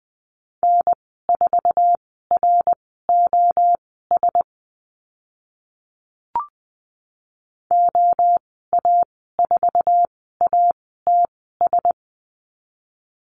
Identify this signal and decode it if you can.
Morse code, heard as a keyed tone, about 20 wpm, spaced as for 14 wpm, 700 Hz. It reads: N4ROS OA4ATS
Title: 069 Lesson 12g - Letter d - Call Signs 20wpm